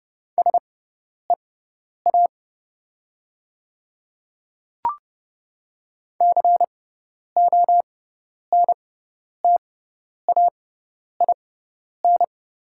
Morse code, heard as a keyed tone, about 30 wpm, spaced as for 10 wpm, 700 Hz. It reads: SEA CONTAIN
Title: 6x Character Spacing - Top 500 Words - 30wpm